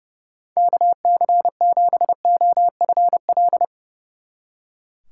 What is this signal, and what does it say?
Morse code, heard as a keyed tone, about 37 wpm, 700 Hz. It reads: KC7OFL
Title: Speed Racing - USA 2x3 Call Signs 30wpm